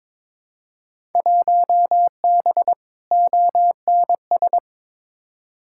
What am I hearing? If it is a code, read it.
Morse code, heard as a keyed tone, about 21 wpm, 700 Hz. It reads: 1B ONS
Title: Field Day Exchanges Encore - 22wpm